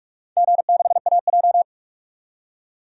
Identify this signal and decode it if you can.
Morse code, heard as a keyed tone, about 44 wpm, 700 Hz. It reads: G6AJ
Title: DXCC-100 Call Signs - 4 Characters - 45wpm